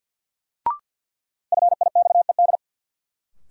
Morse code, heard as a keyed tone, about 53 wpm, 700 Hz. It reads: FIXED
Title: Sentences from Top 1100 Words - New Word Drill Alt - 50wpm